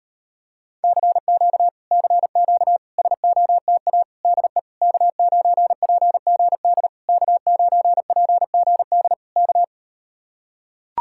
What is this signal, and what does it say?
Morse code, heard as a keyed tone, about 38 wpm, 700 Hz. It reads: CQ CQ SOTA DE K9PGD K9PGD K